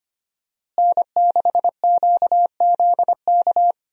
Morse code, heard as a keyed tone, about 25 wpm, 700 Hz. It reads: N6QZK
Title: USA Call Signs - 5 Characters - Repeated 3x - 25wpm